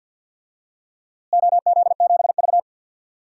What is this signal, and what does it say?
Morse code, heard as a keyed tone, about 49 wpm, 700 Hz. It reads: OZ7V
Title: DXCC-100 Call Signs - 4 Characters - Repeated 3x - 50wpm